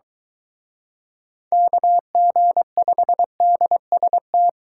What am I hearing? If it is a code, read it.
Morse code, heard as a keyed tone, about 23 wpm, 700 Hz. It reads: KG5DST